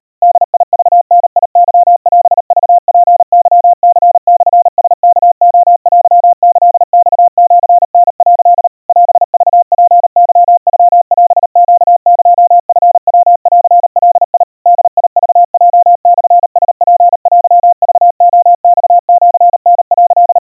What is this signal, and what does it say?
Morse code (Japanese Wabun code), heard as a keyed tone, about 38 wpm, 700 Hz. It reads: ボクダケオクツケニモラワレテキマシタ」 オクツケノオネエチヤンガ ボクヲモラツテクレマシタ」
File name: kawa / Morse sample / kamyu 140 cpm 700 Hz Crispy